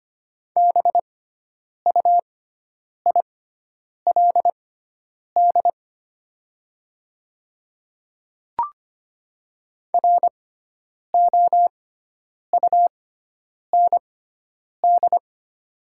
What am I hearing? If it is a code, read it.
Morse code, heard as a keyed tone, about 25 wpm, spaced as for 9 wpm, 700 Hz. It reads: BUILD ROUND
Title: 6x Character Spacing - Top 300 Words - 25wpm